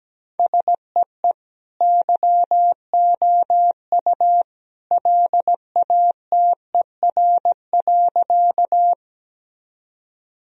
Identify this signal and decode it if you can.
Morse code, heard as a keyed tone, about 17 wpm, 700 Hz. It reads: SEE YOU LATER.
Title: Sentences from Top 1000 Words - Review Part 4 - 17wpm